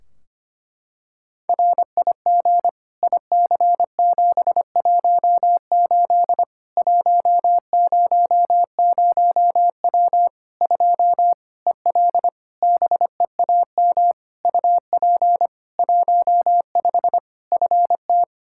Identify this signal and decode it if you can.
Morse code, heard as a keyed tone, about 25 wpm, 700 Hz. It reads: RIG IC718 100W 2 EL BEAM UP 15 FT
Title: Practice QSO - Exchange 2 - Info - 25wpm